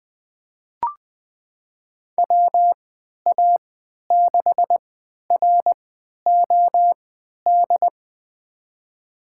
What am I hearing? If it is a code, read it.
Morse code, heard as a keyed tone, about 20 wpm, spaced as for 11 wpm, 700 Hz. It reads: WA6ROD
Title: USA Call Signs - 6 Characters - 3x Character Spacing - 20wpm